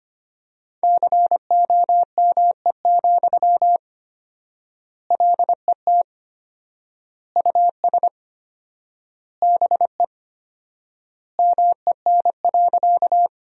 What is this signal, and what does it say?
Morse code, heard as a keyed tone, about 25 wpm, 700 Hz. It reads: COME, LET US BE MEN.
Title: 4x Word-Spacing - Sentences from Top 300 Words - 25wpm